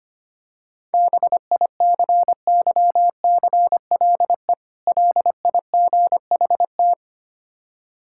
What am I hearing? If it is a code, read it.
Morse code, heard as a keyed tone, about 26 wpm, 690 Hz. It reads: BICYCLE LIGHT
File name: practice / 25wpm / Difficult Words - Speed-Racing with Hints - 25wpm